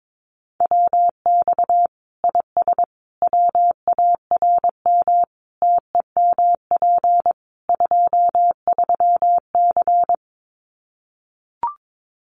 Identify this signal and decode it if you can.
Morse code, heard as a keyed tone, about 22 wpm, 700 Hz. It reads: WX IS WARM TEMP 23C
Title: Common QSO Phrases 22wpm